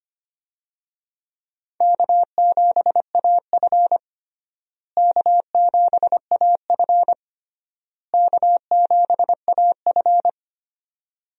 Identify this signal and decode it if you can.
Morse code, heard as a keyed tone, about 25 wpm, 700 Hz. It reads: K7AF K7AF K7AF
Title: USA Call Signs - 4 Characters - Repeated 3x - 25wpm